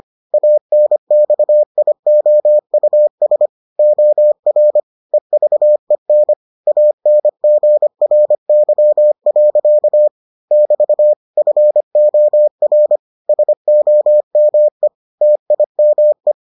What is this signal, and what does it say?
Morse code, heard as a keyed tone, about 25 wpm, 600 Hz. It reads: ANXIOUS OR EVEN ANGRY. <BT> FOR SOME TIME